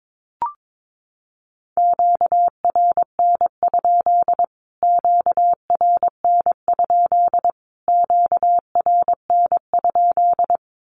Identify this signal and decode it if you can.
Morse code, heard as a keyed tone, about 22 wpm, 700 Hz. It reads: QRN? QRN? QRN?
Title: Q-Codes Repeated 3x 22wpm